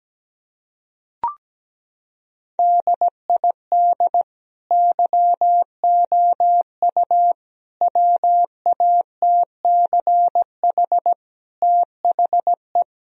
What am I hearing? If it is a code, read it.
Morse code, heard as a keyed tone, about 17 wpm, 700 Hz. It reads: DID YOU WATCH THE
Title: Sentences from Top 1000 Words - Review Part 4 - 17wpm